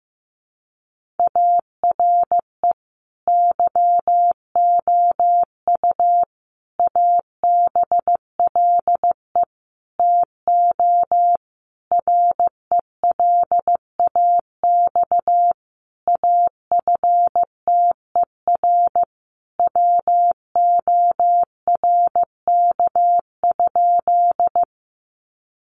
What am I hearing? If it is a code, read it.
Morse code, heard as a keyed tone, about 15 wpm, 700 Hz. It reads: ARE YOU ABLE TO RELAX AFTER WORK?
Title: Sentences from Top 1400 Words 15wpm